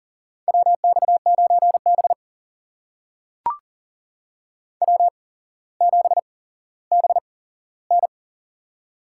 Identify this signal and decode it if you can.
Morse code, heard as a keyed tone, about 42 wpm, spaced as for 17 wpm, 700 Hz. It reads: WX9B W7BN
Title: USA Call Signs - 4 Characters - 8x Character Spacing - 40wpm